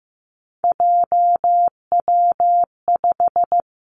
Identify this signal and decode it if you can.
Morse code, heard as a keyed tone, about 15 wpm, 700 Hz. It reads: JW5
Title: Three Challenging Letter-Number Combinations 15wpm